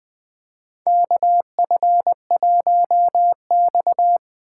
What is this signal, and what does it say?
Morse code, heard as a keyed tone, about 20 wpm, 700 Hz. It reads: KF1X
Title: USA Call Signs - 4 Characters - 2x Character Spacing - 20wpm